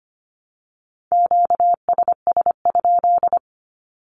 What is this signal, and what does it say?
Morse code, heard as a keyed tone, about 25 wpm, 700 Hz. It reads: QSS?
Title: Q-Codes Repeated 3x 25wpm